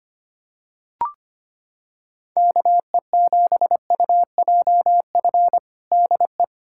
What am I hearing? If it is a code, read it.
Morse code, heard as a keyed tone, about 25 wpm, 700 Hz. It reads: KE7UJF DE
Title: USA Callsign DE Callsign - 6 Characters - 25wpm